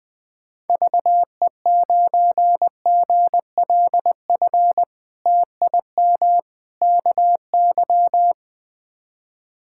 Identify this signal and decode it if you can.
Morse code, heard as a keyed tone, about 21 wpm, 700 Hz. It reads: VE9GLF TIM KY
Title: K1USN SST QSO Practice - 20wpm